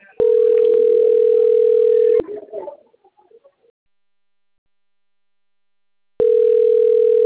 Ringback tone (North America), 440 + 480 Hz.